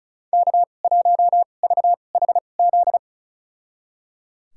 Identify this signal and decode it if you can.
Morse code, heard as a keyed tone, about 36 wpm, spaced as for 25 wpm, 700 Hz. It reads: K1VHZ
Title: USA Call Signs - 2x Character Spacing - 35wpm